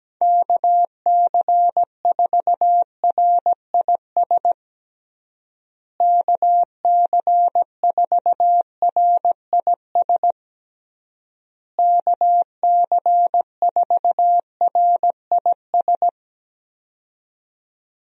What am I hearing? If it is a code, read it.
Morse code, heard as a keyed tone, about 17 wpm, 700 Hz. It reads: KC4RIS KC4RIS KC4RIS
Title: USA Call Signs - 6 Characters - Repeated 3x - 17wpm